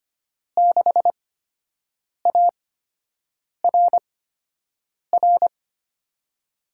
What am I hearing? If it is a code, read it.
Morse code, heard as a keyed tone, about 25 wpm, spaced as for 7 wpm, 700 Hz. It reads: 6ARR
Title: USA Call Signs - 6 Characters - 8x Character Spacing - 25wpm